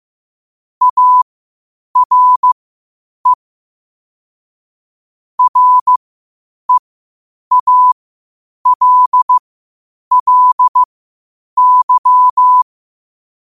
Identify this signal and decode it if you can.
Morse code, heard as a keyed tone, about 15 wpm, spaced as for 8 wpm, 1000 Hz. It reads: ARE REALLY
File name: qotdfiles / 8wpm